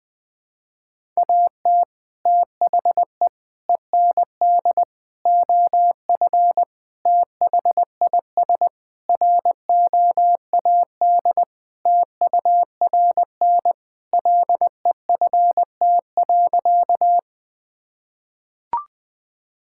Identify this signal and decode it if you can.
Morse code, heard as a keyed tone, about 20 wpm, 700 Hz. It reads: AT THE END OF THIS ROAD TURN LEFT.
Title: Sentences with 5-Word Phrases - Spoken Beginning - 20wpm